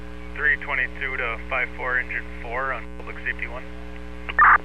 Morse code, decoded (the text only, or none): none